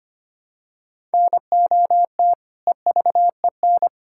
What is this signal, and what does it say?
Morse code, heard as a keyed tone, about 25 wpm, 700 Hz. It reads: NOT EVEN
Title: Sentences from Top 1000 Words - Review Part 6 - 25wpm